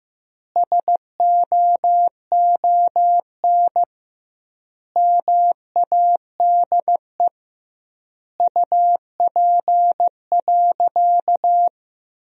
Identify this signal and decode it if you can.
Morse code, heard as a keyed tone, about 15 wpm, 700 Hz. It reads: SOON MADE UP.
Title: 2x Word-Spacing - Sentences from Top 500 Words - 15wpm